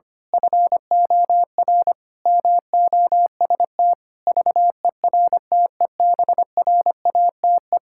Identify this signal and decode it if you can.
Morse code, heard as a keyed tone, about 25 wpm, 700 Hz. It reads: FOR MOST VERTEBRATE
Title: Pangram Challenge v2 - 25wpm